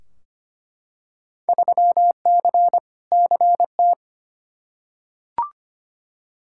Morse code, heard as a keyed tone, about 25 wpm, 700 Hz. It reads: 3C CT
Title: Field Day Exchanges Encore - 25wpm